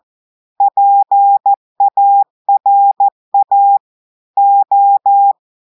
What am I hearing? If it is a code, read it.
Morse code, heard as a keyed tone, about 14 wpm, 800 Hz. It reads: PARA O
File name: audios / portugais / portugais texte en clair 1 vitesse 14 mots minutes